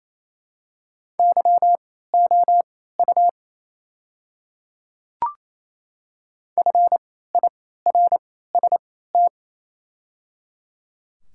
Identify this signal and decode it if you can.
Morse code, heard as a keyed tone, about 27 wpm, spaced as for 16 wpm, 700 Hz. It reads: YOU FIRST
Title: 3x Character Spacing - Top 100 Words - 28wpm